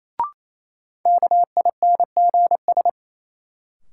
Morse code, heard as a keyed tone, about 28 wpm, 700 Hz. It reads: KINGS
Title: Sentences from Top 1000 Words - Review Part 6 - New Word Drill - 28wpm